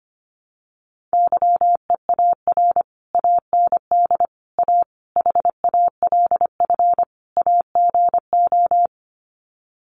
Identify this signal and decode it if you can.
Morse code, heard as a keyed tone, about 25 wpm, 700 Hz. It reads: YEAR AND A HALF AGO
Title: Sets of 5 Words Encore 25wpm